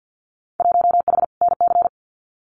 Morse code, heard as a keyed tone, about 48 wpm, 700 Hz. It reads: 1H NC
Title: Winter Field Day Exchanges 50wpm